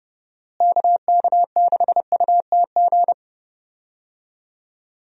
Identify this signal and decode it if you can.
Morse code, heard as a keyed tone, about 30 wpm, 700 Hz. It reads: KK6UTG
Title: USA Call Signs - 6 Characters - Repeated 3x - 30wpm